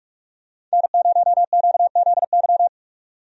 Morse code, heard as a keyed tone, about 45 wpm, 700 Hz. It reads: N0QZY
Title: USA Call Signs - 5 Characters - 45wpm